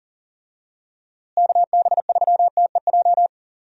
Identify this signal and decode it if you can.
Morse code, heard as a keyed tone, about 40 wpm, 700 Hz. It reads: KB3TEJ
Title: USA Call Signs - 6 Characters - Repeated 2x - 40wpm